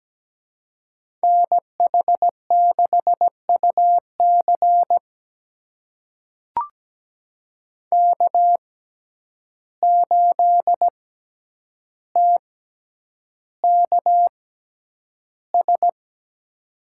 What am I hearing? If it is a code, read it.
Morse code, heard as a keyed tone, about 17 wpm, spaced as for 9 wpm, 700 Hz. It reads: NH6UC K8TKS
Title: USA Call Signs - 6x Character Spacing - 17wpm